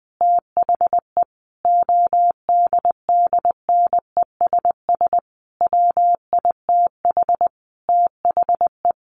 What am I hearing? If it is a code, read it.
Morse code, heard as a keyed tone, about 20 wpm, 700 Hz. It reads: THE ODDNESS WITH THE
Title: A Sound Of Thunder 20wpm